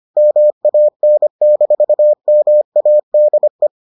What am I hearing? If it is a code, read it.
Morse code, heard as a keyed tone, about 25 wpm, 600 Hz. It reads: MAN-MADE